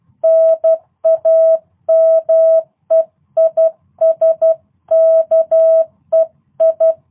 Morse code, heard as a keyed tone, about 12 wpm, 650 Hz. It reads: NAME IS KEI
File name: jh8cbh / 2022cw